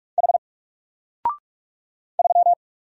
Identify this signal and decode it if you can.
Morse code, heard as a keyed tone, about 45 wpm, 700 Hz. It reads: H 3
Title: Single Letter-Number 45wpm